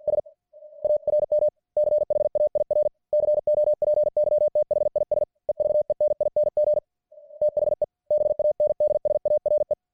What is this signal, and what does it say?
Morse code, heard as a keyed tone, about 50 wpm, 600 Hz. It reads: S ARM CHAIR COPY THIS EVENING THE BANDS ARE